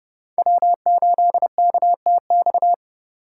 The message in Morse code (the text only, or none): W8KTX